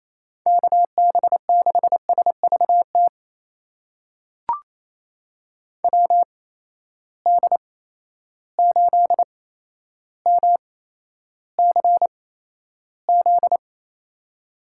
Morse code, heard as a keyed tone, about 28 wpm, spaced as for 12 wpm, 700 Hz. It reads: KB6SVT WD8MCZ
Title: USA Call Signs - 6 Characters - 8x Character Spacing - 28wpm